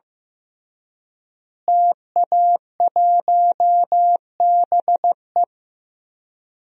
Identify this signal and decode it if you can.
Morse code, heard as a keyed tone, about 15 wpm, 700 Hz. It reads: TA1BE
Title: DXCC-100 Call Signs - 5 Characters - Repeated 3x - 15wpm